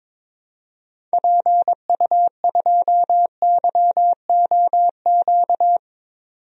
Morse code, heard as a keyed tone, about 22 wpm, 700 Hz. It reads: PU2YOQ
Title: DXCC-100 Call Signs - 6 Characters - 22wpm